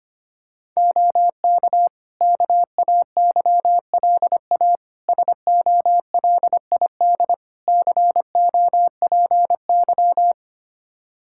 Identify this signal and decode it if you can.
Morse code, heard as a keyed tone, about 25 wpm, 700 Hz. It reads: OK KAYLA SOLID COPY
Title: K1USN SST QSO Practice 25wpm